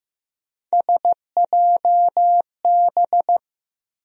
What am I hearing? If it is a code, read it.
Morse code, heard as a keyed tone, about 15 wpm, 700 Hz. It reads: SJB
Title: USA Call Sign Postfixes - 3 Letters 15wpm